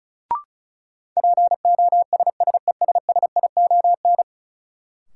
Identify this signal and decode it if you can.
Morse code, heard as a keyed tone, about 36 wpm, 700 Hz. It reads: POSSESSION
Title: Sentences from Top 1400 Words - New Word Drill Alt - 35wpm